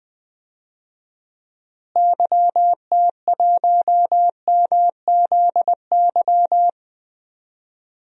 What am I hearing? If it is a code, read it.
Morse code, heard as a keyed tone, about 20 wpm, 700 Hz. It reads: YT1MZY
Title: DXCC-100 Call Signs - 6 Characters - Repeated 2x - 20wpm